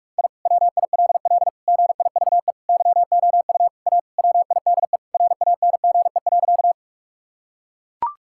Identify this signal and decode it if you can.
Morse code, heard as a keyed tone, about 41 wpm, 700 Hz. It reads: I WILL GIVE YOU A WIDE RANGE.
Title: Sentences from Top 1000 Words - Review Part 6 - 45wpm